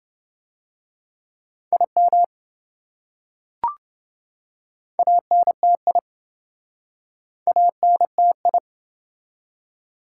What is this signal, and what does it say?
Morse code, heard as a keyed tone, about 26 wpm, 700 Hz. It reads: IM ANTI ANTI